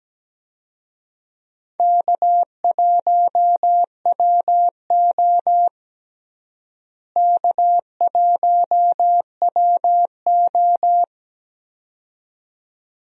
Morse code, heard as a keyed tone, about 17 wpm, 700 Hz. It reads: K1WO K1WO
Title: USA Call Signs - 4 Characters - Repeated 2x - 17wpm